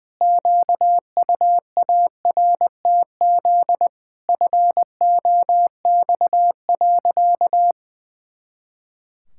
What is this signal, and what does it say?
Morse code, heard as a keyed tone, about 19 wpm, 700 Hz. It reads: QUARTZ FOX.
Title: Pangram Challenge v2 - 20wpm